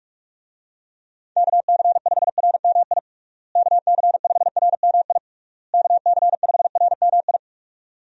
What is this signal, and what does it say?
Morse code, heard as a keyed tone, about 46 wpm, 700 Hz. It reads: KC5RMI KC5RMI KC5RMI
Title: USA Call Signs - 6 Characters - Repeated 3x - 45wpm